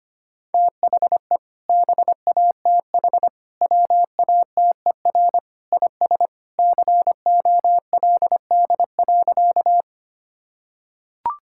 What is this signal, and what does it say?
Morse code, heard as a keyed tone, about 25 wpm, 700 Hz. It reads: THE BATH WATER IS COLD.
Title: Sentences from Top 1600 Words 25wpm